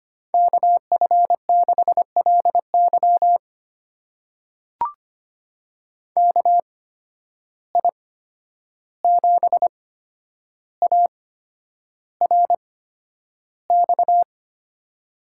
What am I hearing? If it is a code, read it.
Morse code, heard as a keyed tone, about 25 wpm, spaced as for 10 wpm, 700 Hz. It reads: KF6LY KI7ARX
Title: USA Call Signs - 8x Character Spacing - 25wpm